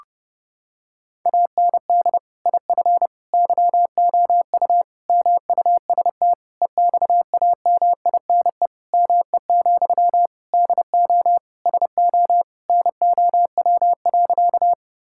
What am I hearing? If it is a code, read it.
Morse code, heard as a keyed tone, about 30 wpm, 700 Hz. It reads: AND IF YOU MUST EXAMINE ME, DO SO NOW.
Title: Sentences from Top 1600 Words 30wpm